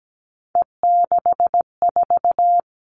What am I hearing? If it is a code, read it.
Morse code, heard as a keyed tone, about 17 wpm, 700 Hz. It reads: E64
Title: All Cut Numbers - 3-Digit 17wpm